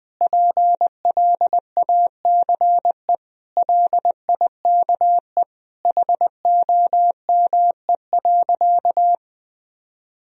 Morse code, heard as a keyed tone, about 21 wpm, 700 Hz. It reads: PLACE LIKE HOME.
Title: Sentences from Top 200 Words and Abbreviations 20wpm